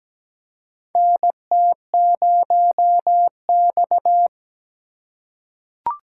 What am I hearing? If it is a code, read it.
Morse code, heard as a keyed tone, about 17 wpm, 700 Hz. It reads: NT0X
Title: USA Call Signs - 4 Characters - 4x Character Spacing - 17wpm